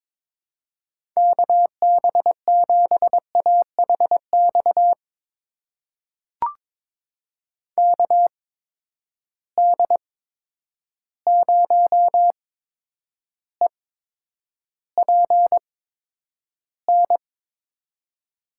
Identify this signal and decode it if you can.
Morse code, heard as a keyed tone, about 22 wpm, spaced as for 9 wpm, 700 Hz. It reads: KB7AHX KD0EPN
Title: USA Call Signs - 6 Characters - 8x Character Spacing - 22wpm